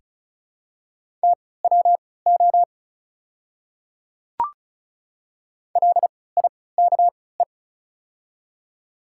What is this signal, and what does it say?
Morse code, heard as a keyed tone, about 36 wpm, spaced as for 20 wpm, 690 Hz. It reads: TWO LIKE
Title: 3x Character Spacing - Top 100 Words - 35wpm